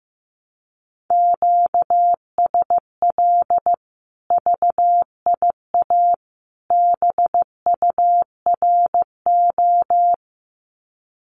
Morse code, heard as a keyed tone, about 15 wpm, 700 Hz. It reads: QSL VIA BURO
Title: Common QSO Phrases 15wpm